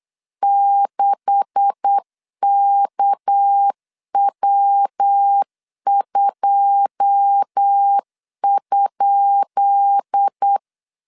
Morse code, heard as a keyed tone, about 8 wpm, 800 Hz. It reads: 6KW2?
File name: LEZIONI / 10 SIGNS / 10B INT 8wpm 3534